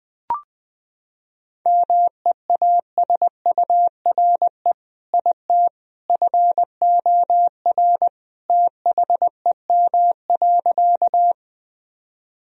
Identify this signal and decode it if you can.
Morse code, heard as a keyed tone, about 20 wpm, 700 Hz. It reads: MEASURE IT FOR THEM.
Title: Sentences from Top 1000 Words - Review Part 2 - 20wpm